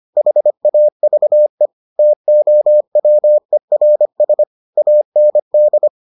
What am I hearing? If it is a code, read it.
Morse code, heard as a keyed tone, about 25 wpm, 600 Hz. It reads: HAVE TOWERS AND